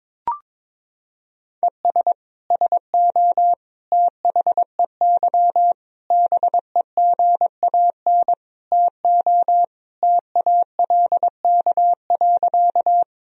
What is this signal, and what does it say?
Morse code, heard as a keyed tone, about 21 wpm, 700 Hz. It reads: ES SO THEY BEGAN TO TALK.
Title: Sentences from Top 400 Words and Abbreviations 22wpm